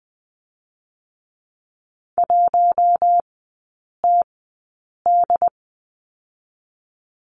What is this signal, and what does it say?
Morse code, heard as a keyed tone, about 20 wpm, spaced as for 8 wpm, 700 Hz. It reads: 1TD
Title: 081 Lesson 14e - Number 2 - Three Characters 20wpm